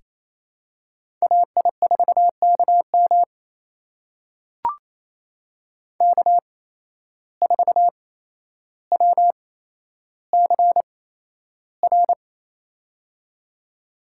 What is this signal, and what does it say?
Morse code, heard as a keyed tone, about 28 wpm, spaced as for 12 wpm, 700 Hz. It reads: AI4KM K4WCR
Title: USA Call Signs - 5 Characters - 8x Character Spacing - 28wpm